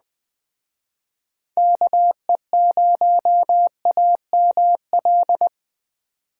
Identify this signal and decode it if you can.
Morse code, heard as a keyed tone, about 20 wpm, 700 Hz. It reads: KE0AML